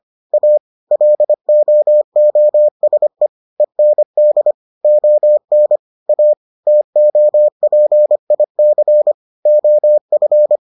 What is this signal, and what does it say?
Morse code, heard as a keyed tone, about 25 wpm, 600 Hz. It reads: A LOOSE END ON A TOPIC OF